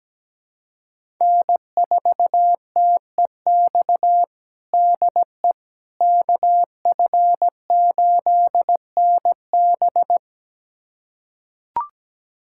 Morse code, heard as a keyed tone, about 17 wpm, 700 Hz. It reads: N4TEX DE KF8NB
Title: USA Callsign DE Callsign - 5 Characters - 17wpm